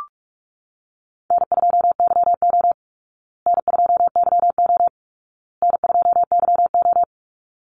Morse code, heard as a keyed tone, about 44 wpm, 700 Hz. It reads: N2YO N2YO N2YO
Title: Easy Call Signs Repeated 3x - 45wpm